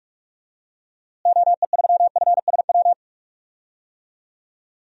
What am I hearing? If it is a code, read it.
Morse code, heard as a keyed tone, about 43 wpm, 700 Hz. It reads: OE3FSW
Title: DXCC-100 Call Signs - 6 Characters - Repeated 3x - 45wpm